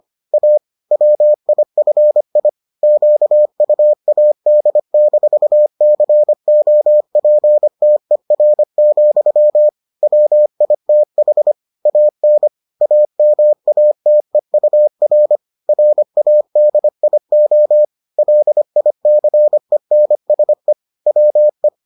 Morse code, heard as a keyed tone, about 25 wpm, 600 Hz. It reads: A WIFI QUAD-COPTER, WITH AN AMATEUR RADIO LICENSE WE